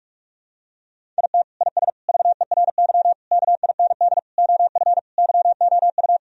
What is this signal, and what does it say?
Morse code, heard as a keyed tone, about 42 wpm, 700 Hz. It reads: IT IS VERY KIND OF YOU
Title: Sentences from Top 1000 Words - Review Part 1 - 45wpm